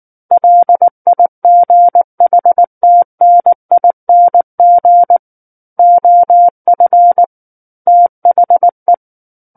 Morse code, heard as a keyed tone, about 19 wpm, 700 Hz. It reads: LIGHTNING OF THE
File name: Book One / 10-Chapter 10 - 19 WPM